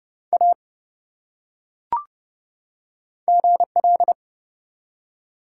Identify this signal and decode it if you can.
Morse code, heard as a keyed tone, about 30 wpm, 700 Hz. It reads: A GL
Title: Top 180 QSO Elements 30wpm